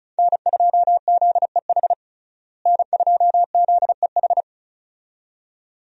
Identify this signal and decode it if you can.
Morse code, heard as a keyed tone, about 34 wpm, 700 Hz. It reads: N2ZEH N2ZEH
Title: USA Call Signs - 5 Characters - Repeated 2x - 35wpm